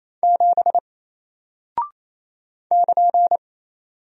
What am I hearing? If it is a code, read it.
Morse code, heard as a keyed tone, about 28 wpm, 700 Hz. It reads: 7 <KN>
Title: Single Character (Letter:Number:Punctuation:Prosign) Alt 28wpm